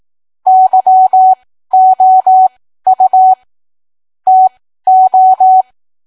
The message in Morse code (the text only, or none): YOU TO